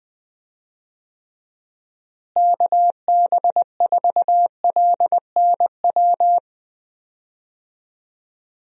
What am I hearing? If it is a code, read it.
Morse code, heard as a keyed tone, about 20 wpm, 700 Hz. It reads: KB4LNW
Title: USA Call Signs - 6 Characters - Repeated 2x - 20wpm